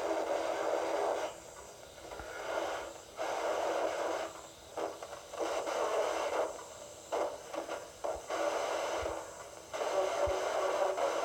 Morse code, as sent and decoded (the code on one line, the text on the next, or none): none